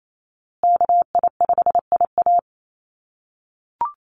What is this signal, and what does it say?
Morse code, heard as a keyed tone, about 28 wpm, 700 Hz. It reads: KI5IA
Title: USA Call Signs Repeated 3x 28wpm